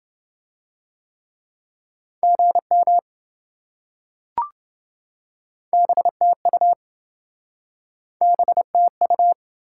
Morse code, heard as a keyed tone, about 27 wpm, 700 Hz. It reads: GM BTU BTU